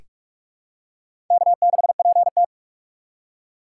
Morse code, heard as a keyed tone, about 46 wpm, 700 Hz. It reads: K6PT
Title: USA Call Signs - 4 Characters - 4x Character Spacing - 45wpm